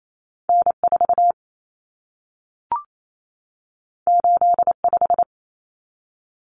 Morse code, heard as a keyed tone, about 28 wpm, 700 Hz. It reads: N4 85
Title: Common Cut Numbers - 2-Digit 28wpm